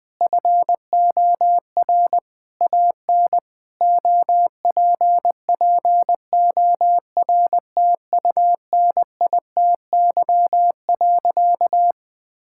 Morse code, heard as a keyed tone, about 19 wpm, 700 Hz. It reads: FOR AN OPPORTUNITY.
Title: Sentences with 6-Word Phrases - 20wpm